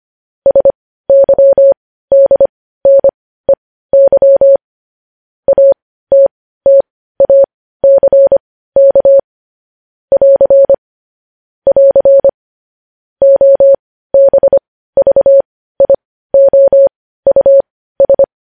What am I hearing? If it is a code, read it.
Morse code, heard as a keyed tone, about 25 wpm, spaced as for 15 wpm, 550 Hz. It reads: SYDNEY ATTACK <AR> <AR> OBVIOUS
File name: podcast / News headlines-1x-25x15